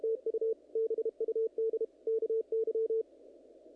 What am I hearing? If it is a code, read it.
Morse code, heard as a keyed tone, about 30 wpm, 470 Hz. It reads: TU BUD KY
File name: contests / naqp-cw-k5tr-op-k5pi- -1900